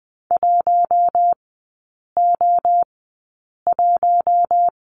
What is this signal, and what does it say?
Morse code, heard as a keyed tone, about 20 wpm, spaced as for 8 wpm, 700 Hz. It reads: 1O1
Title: 041 Lesson 8e - Number 1 - Three Characters 20wpm